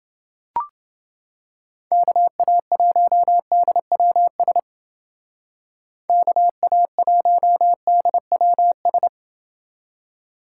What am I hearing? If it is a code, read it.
Morse code, heard as a keyed tone, about 29 wpm, 700 Hz. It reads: KA1DWS KA1DWS